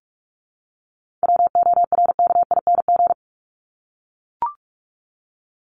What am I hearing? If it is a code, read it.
Morse code, heard as a keyed tone, about 44 wpm, 700 Hz. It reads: WORKING
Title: Top 501-750 Words 45wpm